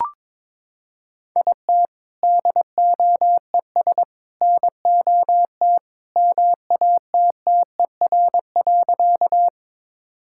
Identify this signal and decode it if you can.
Morse code, heard as a keyed tone, about 22 wpm, 700 Hz. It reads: IT DOES NOT MATTER.
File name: practice / 22wpm / Sentences with 3-Word Phrases - 22wpm